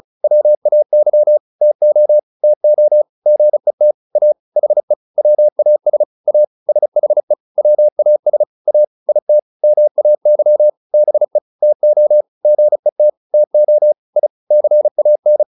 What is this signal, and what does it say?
Morse code, heard as a keyed tone, about 34 wpm, 600 Hz. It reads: WAY TO TO GET A HE WAS A SHE WAS A IT MAY BE TO GET TO I CAN
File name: WD201 35